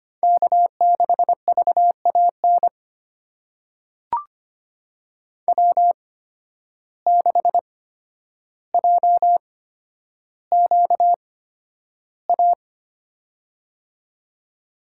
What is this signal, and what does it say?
Morse code, heard as a keyed tone, about 25 wpm, spaced as for 11 wpm, 700 Hz. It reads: K6VAN W6JQA
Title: USA Call Signs - 5 Characters - 8x Character Spacing - 25wpm